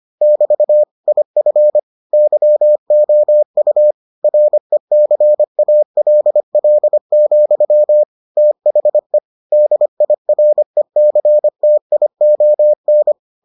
Morse code, heard as a keyed tone, about 25 wpm, 600 Hz. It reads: <BT> IF YOU RECALL, THE DIRECTION